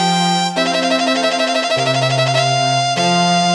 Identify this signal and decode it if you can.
There is no code.